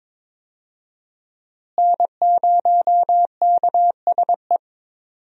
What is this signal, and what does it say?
Morse code, heard as a keyed tone, about 22 wpm, 700 Hz. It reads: N0KSE
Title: USA Call Signs - 5 Characters - Repeated 2x - 22wpm